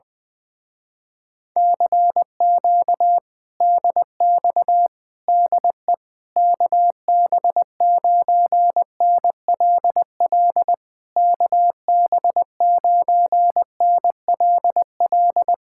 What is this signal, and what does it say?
Morse code, heard as a keyed tone, about 20 wpm, 700 Hz. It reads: CQ DX DE KB9NLL KB9NLL
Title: Speed-Racing Common QSO Phrases - 17wpm